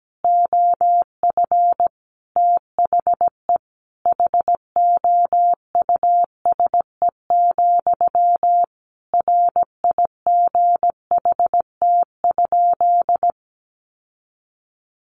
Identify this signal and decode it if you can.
Morse code, heard as a keyed tone, about 17 wpm, 700 Hz. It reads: OF THE HOUSE, RIGHT?
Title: Sentences from Top 1200 Words 17wpm